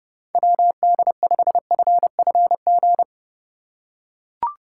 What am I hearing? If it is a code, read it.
Morse code, heard as a keyed tone, about 32 wpm, 700 Hz. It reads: WD5FFG
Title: USA Call Signs - 6 Characters - 2x Character Spacing - 30wpm